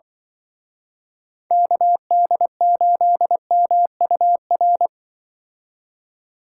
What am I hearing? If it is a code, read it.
Morse code, heard as a keyed tone, about 24 wpm, 700 Hz. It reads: KD8MUR
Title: Speed Racing - USA Call Signs 20wpm